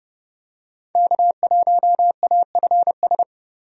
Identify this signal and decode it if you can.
Morse code, heard as a keyed tone, about 29 wpm, 700 Hz. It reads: K1AFS
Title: USA Call Signs - 5 Characters - 30wpm